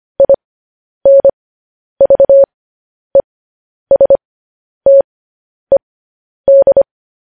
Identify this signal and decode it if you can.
Morse code, heard as a keyed tone, about 25 wpm, spaced as for 10 wpm, 550 Hz. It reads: INVESTED